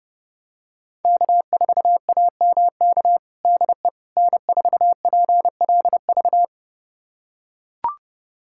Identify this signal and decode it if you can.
Morse code, heard as a keyed tone, about 32 wpm, 700 Hz. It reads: K4AMK DE N4PLV
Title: USA Callsign DE Callsign - 5 Characters - 30wpm